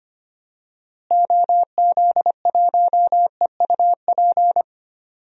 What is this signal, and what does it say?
Morse code, heard as a keyed tone, about 25 wpm, 700 Hz. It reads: OZ1EUP
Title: DXCC-100 Call Signs - 6 Characters - Repeated 2x - 25wpm